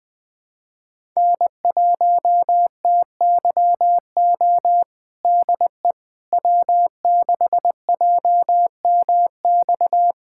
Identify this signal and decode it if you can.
Morse code, heard as a keyed tone, about 20 wpm, 700 Hz. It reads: N1TYO DE W6JMX
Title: USA Callsign DE Callsign - 5 Characters - 20wpm